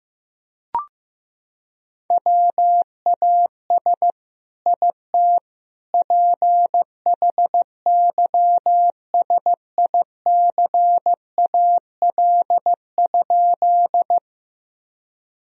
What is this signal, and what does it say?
Morse code, heard as a keyed tone, about 16 wpm, 700 Hz. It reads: WAS IT PHYSICAL?
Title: Sentences from Top 1000 Words - Review Part 4 - 15wpm